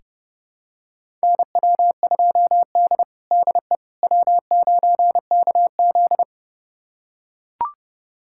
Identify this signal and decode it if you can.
Morse code, heard as a keyed tone, about 31 wpm, 700 Hz. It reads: NW2D DE W9KZ
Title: USA Callsign DE Callsign - 4 Characters - 30wpm